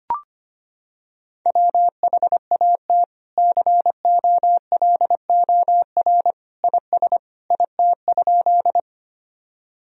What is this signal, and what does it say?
Morse code, heard as a keyed tone, about 25 wpm, 700 Hz. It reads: WHAT COLOR IS IT?
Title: Sentences from Top 1000 Words - Review Part 2 - 25wpm